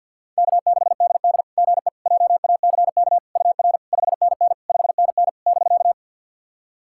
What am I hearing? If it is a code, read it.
Morse code, heard as a keyed tone, about 50 wpm, 690 Hz. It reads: K6DD GE JACK UR 5NN 5NN <BK>
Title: SOTA QSO Practice 50wpm